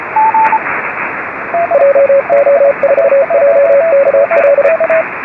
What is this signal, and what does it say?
Morse code, heard as a keyed tone, about 32 wpm, 560 Hz, with radio noise. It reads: JO4JKL
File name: RS44 JA9KRO